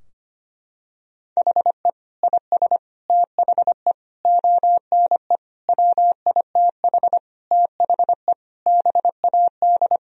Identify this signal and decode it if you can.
Morse code, heard as a keyed tone, about 25 wpm, 700 Hz. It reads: HE IS THE ONE WITH THE BAD
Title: Sentences from Top 1000 Words - Review Part 3 - 25wpm